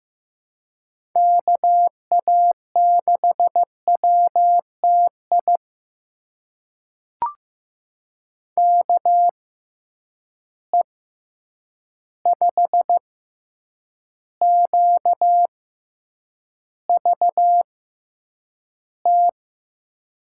Morse code, heard as a keyed tone, about 15 wpm, spaced as for 8 wpm, 700 Hz. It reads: KA6WTI KE5QVT
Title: USA Call Signs - 6 Characters - 6x Character Spacing - 15wpm